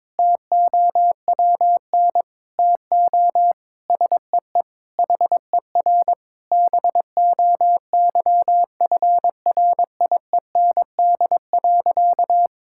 Morse code, heard as a keyed tone, about 22 wpm, 700 Hz. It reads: TOWN TO SEE HER BOYFRIEND.
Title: Sentences from Top 1000 Words - Review Part 6 - 22wpm